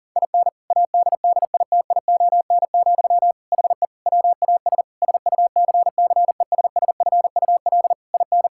Morse code, heard as a keyed tone, about 37 wpm, 700 Hz. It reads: IN ADDITION, HE WAS SUCCESSFUL IN
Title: Sentences from Top 1000 Words - Review Part 6 - 40wpm